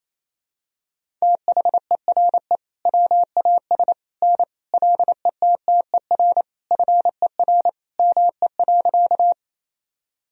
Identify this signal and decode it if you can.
Morse code, heard as a keyed tone, about 28 wpm, 700 Hz. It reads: THERE WAS N LETTER FER ME.
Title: Sentences from Top 300 Words and Abbreviations 28wpm